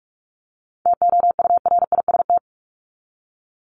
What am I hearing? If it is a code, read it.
Morse code, heard as a keyed tone, about 44 wpm, 700 Hz. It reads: TOURIST
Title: Top 1501-1750 Words 45wpm